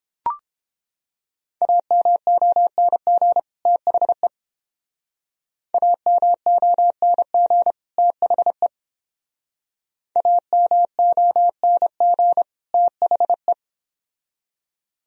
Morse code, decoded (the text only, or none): AMONG THE AMONG THE AMONG THE